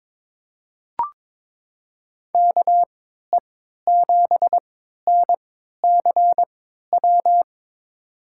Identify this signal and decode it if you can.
Morse code, heard as a keyed tone, about 22 wpm, spaced as for 13 wpm, 700 Hz. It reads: KE7NCW
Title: USA Call Signs - 6 Characters - 3x Character Spacing - 22wpm